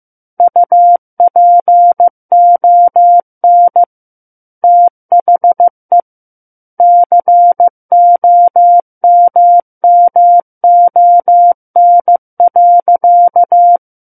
Morse code, heard as a keyed tone, about 15 wpm, 700 Hz. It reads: UPON THE COMMON.
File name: Chapter 6 - 15 WPM